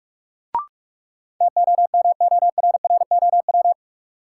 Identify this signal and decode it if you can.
Morse code, heard as a keyed tone, about 44 wpm, 700 Hz. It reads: TOMORROW
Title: Sentences from Top 1100 Words - New Word Drill - 45wpm